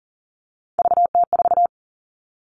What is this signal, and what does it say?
Morse code, heard as a keyed tone, about 40 wpm, 700 Hz. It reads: VT4
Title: All Cut Numbers - 3-Digit 40wpm